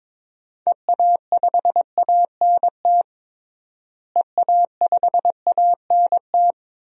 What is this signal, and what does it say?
Morse code, heard as a keyed tone, about 22 wpm, 700 Hz. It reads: EA5ANT EA5ANT
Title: DXCC-100 Call Signs - 6 Characters - Repeated 2x - 22wpm